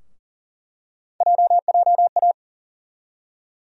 Morse code, heard as a keyed tone, about 40 wpm, 700 Hz. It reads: JJA
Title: USA Call Sign Postfixes - 3 Letters 40wpm